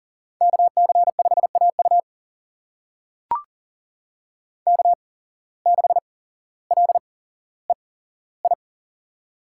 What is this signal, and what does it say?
Morse code, heard as a keyed tone, about 40 wpm, spaced as for 17 wpm, 700 Hz. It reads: KC5AU K6LEI
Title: USA Call Signs - 5 Characters - 8x Character Spacing - 40wpm